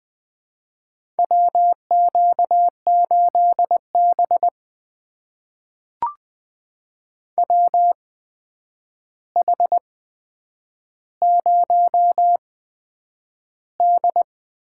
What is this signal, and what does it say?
Morse code, heard as a keyed tone, about 20 wpm, spaced as for 9 wpm, 700 Hz. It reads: WQ8B WH0D
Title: USA Call Signs - 4 Characters - 8x Character Spacing - 20wpm